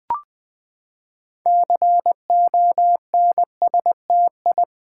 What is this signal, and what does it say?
Morse code, heard as a keyed tone, about 20 wpm, 700 Hz. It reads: CONSTI
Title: Build A Word - 6 Letters At A Time - 12-Letter Words - 20wpm